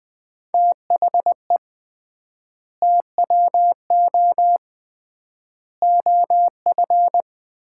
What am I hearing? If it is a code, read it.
Morse code, heard as a keyed tone, about 20 wpm, 700 Hz. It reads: THE TWO OF
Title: 3x Word-Spacing - Sets of 4 Words - 20wpm